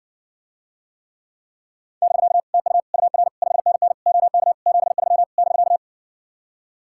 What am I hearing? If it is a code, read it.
Morse code, heard as a keyed tone, about 66 wpm, 700 Hz. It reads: <BK> TU UR 5NN OK 73 <BK>